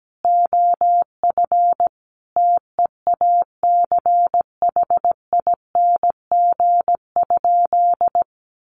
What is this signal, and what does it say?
Morse code, heard as a keyed tone, about 17 wpm, 700 Hz. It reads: OF TEACHING?
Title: Sentences from Top 1100 Words 17wpm